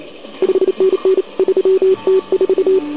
Morse code, heard as a keyed tone, about 30 wpm, 380 Hz. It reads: 5NN 3T4
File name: ARCHIV / OKDX2019 / OKOM40 KX3